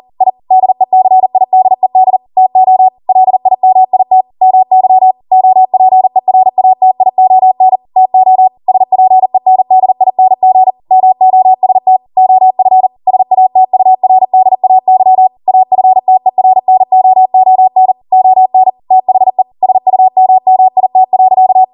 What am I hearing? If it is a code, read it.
Morse code, heard as a keyed tone, about 40 wpm, 750 Hz. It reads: I DECIDED TO LIMIT MY OPERATION TO SPENDING MOST OF SATURDAY AFTERNOON ON THE SUMMIT.